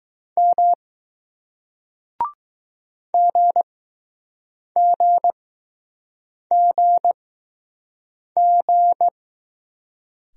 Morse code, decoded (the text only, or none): M GGGG